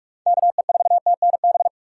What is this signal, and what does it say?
Morse code, heard as a keyed tone, about 46 wpm, 700 Hz. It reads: KE4TNB
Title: USA Call Signs - 6 Characters - 2x Character Spacing - 45wpm